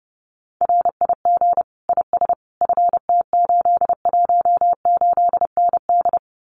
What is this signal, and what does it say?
Morse code, heard as a keyed tone, about 30 wpm, 700 Hz. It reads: RIG IS FT818ND
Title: Common QSO Phrases Encore 30wpm